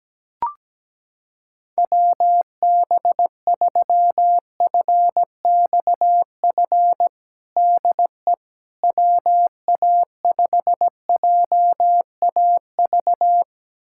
Morse code, heard as a keyed tone, about 17 wpm, 700 Hz. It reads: WB3FXF DE WA5JAV
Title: USA Callsign DE Callsign 17wpm